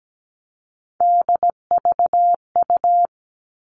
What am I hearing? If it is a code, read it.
Morse code, heard as a keyed tone, about 17 wpm, 700 Hz. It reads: DVU
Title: All Cut Numbers - 3-Digit 17wpm